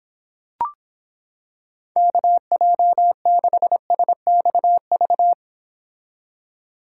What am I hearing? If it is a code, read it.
Morse code, heard as a keyed tone, about 26 wpm, 700 Hz. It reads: KJ6SXV